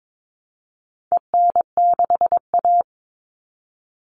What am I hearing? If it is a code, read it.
Morse code, heard as a keyed tone, about 22 wpm, 700 Hz. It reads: EN6A
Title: All Cut Numbers - 4-Digit 22wpm